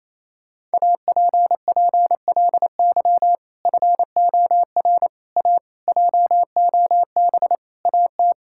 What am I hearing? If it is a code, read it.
Morse code, heard as a keyed tone, about 28 wpm, 700 Hz. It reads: APPLY FOR A JOB AT
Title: Sentences from Top 1600 Words 28wpm